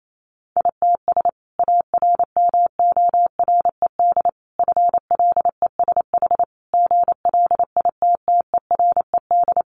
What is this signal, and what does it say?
Morse code, heard as a keyed tone, about 28 wpm, 700 Hz. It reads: ITS ARMORED FLESH GLITTERED
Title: A Sound Of Thunder 28wpm